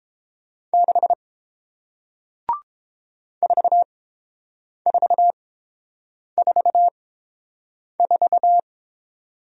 Morse code, heard as a keyed tone, about 30 wpm, spaced as for 7 wpm, 700 Hz. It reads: 6 4444